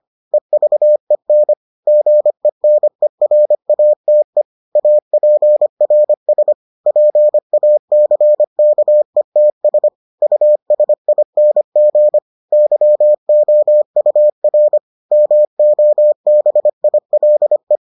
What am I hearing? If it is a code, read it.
Morse code, heard as a keyed tone, about 25 wpm, 600 Hz. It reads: EVEN GENERATE APRS PACKETS USING YOUR MOBILE